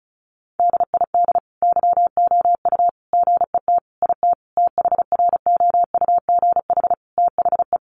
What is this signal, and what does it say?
Morse code, heard as a keyed tone, about 34 wpm, 700 Hz. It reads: DID YOU GET IT THROUGH THE
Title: Sentences from Top 1300 Words 35wpm